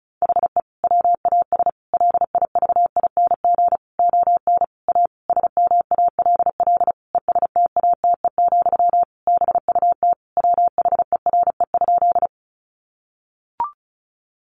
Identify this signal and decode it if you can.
Morse code, heard as a keyed tone, about 34 wpm, 700 Hz. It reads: HE WAS LIVING ON A SMALL ESTATE, BUT WHERE?
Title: Sentences from Top 1200 Words 35wpm